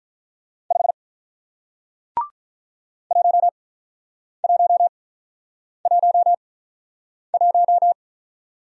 Morse code, heard as a keyed tone, about 48 wpm, spaced as for 9 wpm, 700 Hz. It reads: 5 1111